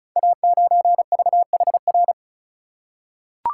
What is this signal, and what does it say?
Morse code, heard as a keyed tone, about 34 wpm, 700 Hz. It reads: A9VHR
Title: Code Groups - 5 Letter:Numbers - 35wpm